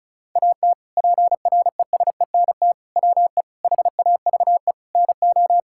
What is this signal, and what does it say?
Morse code, heard as a keyed tone, about 36 wpm, 700 Hz. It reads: AT PRESENT WE HAVE NO
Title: Sentences from Top 1000 Words - Review Part 5 - 35wpm